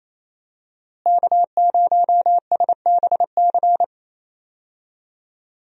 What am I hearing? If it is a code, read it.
Morse code, heard as a keyed tone, about 27 wpm, 700 Hz. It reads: K0SBC
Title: USA Call Signs - 5 Characters - Repeated 3x - 28wpm